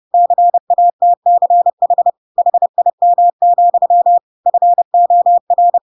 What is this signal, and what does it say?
Morse code, heard as a keyed tone, about 30 wpm, 700 Hz. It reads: CATCH HIM, FOR
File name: Pinocchio / Pinocchio ch3 30 30 ik1hge